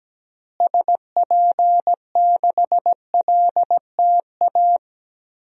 Morse code, heard as a keyed tone, about 17 wpm, 700 Hz. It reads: SP6LTA
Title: DXCC-100 Call Signs - 6 Characters - Repeated 3x - 17wpm